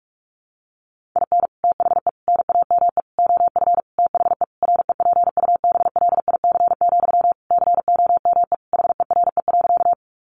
Morse code, heard as a keyed tone, about 44 wpm, 700 Hz. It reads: IN THE NAME OF THE REPUBLIC, COME HERE.
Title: Sentences from Top 1500 Words 45wpm